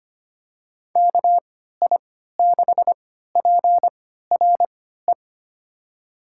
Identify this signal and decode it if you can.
Morse code, heard as a keyed tone, about 25 wpm, spaced as for 14 wpm, 700 Hz. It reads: KI6PRE
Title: USA Call Signs - 6 Characters - 3x Character Spacing - 25wpm